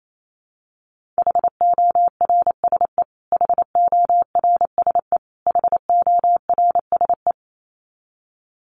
Morse code, heard as a keyed tone, about 28 wpm, 700 Hz. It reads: HORSE HORSE HORSE
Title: Top 300 Words Repeated 28wpm